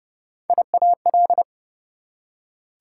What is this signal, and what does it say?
Morse code, heard as a keyed tone, about 29 wpm, 700 Hz. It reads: IAL
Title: Prefixes and Suffixes 30wpm